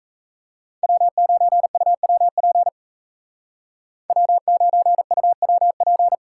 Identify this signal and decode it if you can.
Morse code, heard as a keyed tone, about 34 wpm, 700 Hz. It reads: W9UWP W9UWP
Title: Speed Racing - USA 1x3 Call Signs 28wpm